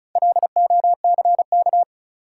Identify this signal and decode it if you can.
Morse code, heard as a keyed tone, about 36 wpm, 700 Hz. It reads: LOCK
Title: Sentences from Top 1300 Words - New Word Drill - 35wpm